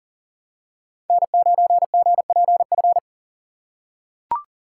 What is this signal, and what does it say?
Morse code, heard as a keyed tone, about 40 wpm, 700 Hz. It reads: N9GPF
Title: USA Call Signs - 4x Character Spacing - 40wpm